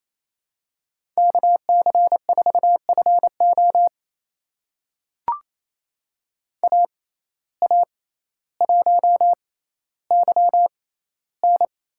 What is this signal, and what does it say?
Morse code, heard as a keyed tone, about 28 wpm, spaced as for 14 wpm, 700 Hz. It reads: KC4FO AA1YN
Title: USA Call Signs - 6x Character Spacing - 28wpm